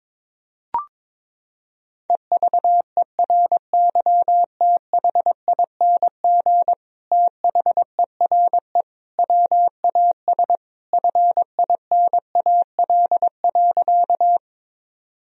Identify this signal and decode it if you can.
Morse code, heard as a keyed tone, about 22 wpm, 700 Hz. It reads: EVERYTHING THERE WAS FINAL.
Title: Sentences from Top 1000 Words - Review Part 2 - 22wpm